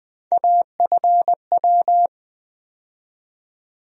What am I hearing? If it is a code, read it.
Morse code, heard as a keyed tone, about 20 wpm, 700 Hz. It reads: AFW
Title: USA Call Sign Postfixes - 3 Letters 20wpm